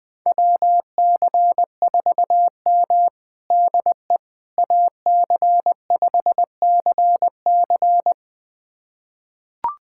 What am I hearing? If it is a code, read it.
Morse code, heard as a keyed tone, about 21 wpm, 700 Hz. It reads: WC4M DE AC5CC
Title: USA Callsign DE Callsign 20wpm